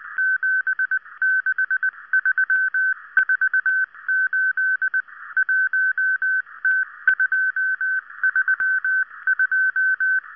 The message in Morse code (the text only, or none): 76348 1T232